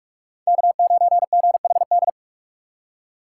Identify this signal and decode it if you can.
Morse code, heard as a keyed tone, about 43 wpm, 700 Hz. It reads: K9GHD
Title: USA Call Signs - 4x Character Spacing - 45wpm